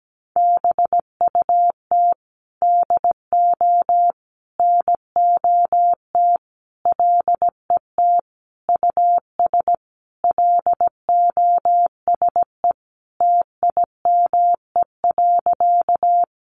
Morse code, heard as a keyed tone, about 17 wpm, 700 Hz. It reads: BUT DO NOT LET US LOSE TIME.
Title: Sentences from Top 1100 Words 17wpm